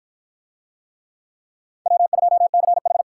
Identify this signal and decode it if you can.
Morse code, heard as a keyed tone, about 51 wpm, 700 Hz. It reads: W2CH